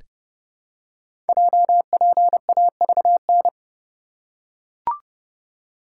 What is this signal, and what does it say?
Morse code, heard as a keyed tone, about 29 wpm, 700 Hz. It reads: JPAVN
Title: Code Groups - 5 Letters - 30wpm